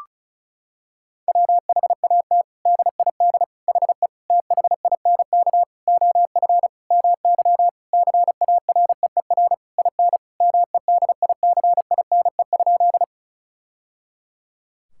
Morse code, heard as a keyed tone, about 34 wpm, 700 Hz. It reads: WHAT DID HE THINK OF MY CAREER IN MEDICINE?
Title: Sentences from Top 1000 Words - Review Part 6 - 35wpm